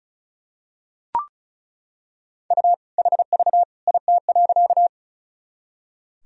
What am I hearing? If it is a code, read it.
Morse code, heard as a keyed tone, about 34 wpm, 700 Hz. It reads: U HV IT.
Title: Sentences from Top 100 Words and Abbreviations 35wpm